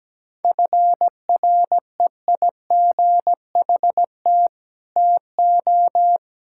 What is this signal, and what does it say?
Morse code, heard as a keyed tone, about 17 wpm, 700 Hz. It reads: FREIGHT TO
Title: Pangram Challenge v2 - 17wpm